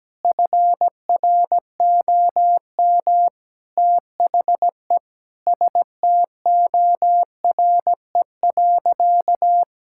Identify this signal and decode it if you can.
Morse code, heard as a keyed tone, about 17 wpm, 700 Hz. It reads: FROM THE STORE.
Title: Sentences from Top 1000 Words - Review Part 5 - 17wpm